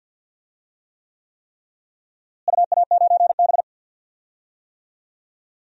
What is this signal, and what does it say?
Morse code, heard as a keyed tone, about 52 wpm, 700 Hz. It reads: UA9B
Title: DXCC-100 Call Signs - 4 Characters - Repeated 3x - 50wpm